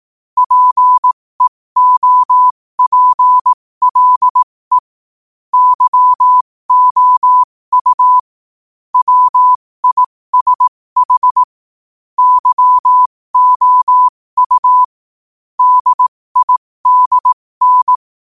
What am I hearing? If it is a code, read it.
Morse code, heard as a keyed tone, about 18 wpm, spaced as for 15 wpm, 1000 Hz. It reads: PEOPLE YOU WISH YOU DIDN
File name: qotdfiles / 15wpm